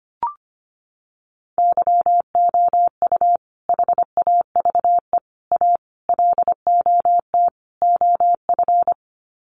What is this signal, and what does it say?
Morse code, heard as a keyed tone, about 25 wpm, 700 Hz. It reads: YOU HAVE A LOT OF
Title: Sets of 5 Words Encore 25wpm